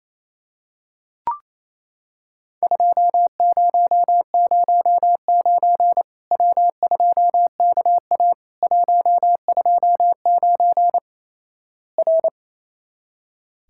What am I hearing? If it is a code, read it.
Morse code, heard as a keyed tone, about 28 wpm, 700 Hz. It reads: 2009 W2KA 129 R
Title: Worked All Europe DX Contest - QTC Exchange - 28wpm